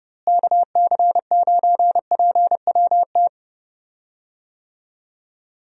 Morse code, heard as a keyed tone, about 30 wpm, 700 Hz. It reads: KC9PWT
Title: USA Call Signs - 6 Characters - Repeated 2x - 30wpm